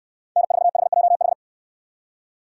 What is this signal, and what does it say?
Morse code, heard as a keyed tone, about 68 wpm, 700 Hz. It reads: N4HJH